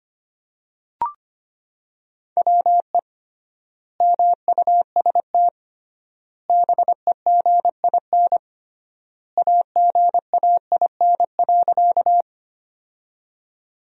Morse code, heard as a keyed tone, about 25 wpm, 700 Hz. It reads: WE MUST BEGIN AGAIN.
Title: 3x Word-Spacing - Sentences from Top 300 Words - 25wpm